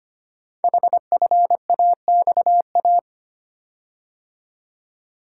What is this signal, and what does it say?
Morse code, heard as a keyed tone, about 25 wpm, 700 Hz. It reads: HFAXA
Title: Code Groups - 5 Letters - 25wpm